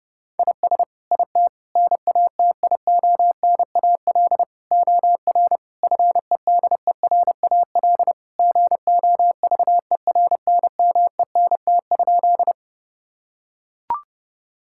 Morse code, heard as a keyed tone, about 32 wpm, 700 Hz. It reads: IS IT NATIONAL OR FEDERAL GOVERNMENT?
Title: Sentences from Top 1000 Words - Review Part 5 - 30wpm